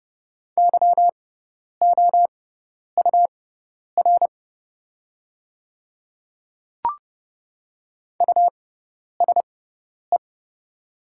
Morse code, heard as a keyed tone, about 30 wpm, spaced as for 10 wpm, 690 Hz. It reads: YOUR USE
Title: 6x Character Spacing - Top 100 Words - 30wpm